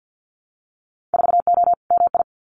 Morse code, heard as a keyed tone, about 50 wpm, 700 Hz. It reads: 4O MI
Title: Winter Field Day Exchanges Encore 50wpm